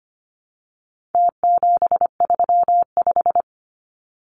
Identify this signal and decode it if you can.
Morse code, heard as a keyed tone, about 25 wpm, 700 Hz. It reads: T735
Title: Numbers - Contest Style - 4-Digit 25wpm